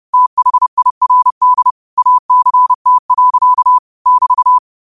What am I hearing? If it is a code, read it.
Morse code, heard as a keyed tone, about 30 wpm, 1000 Hz. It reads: THIRD ACT. <BT>
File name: qotdfiles / 30wpm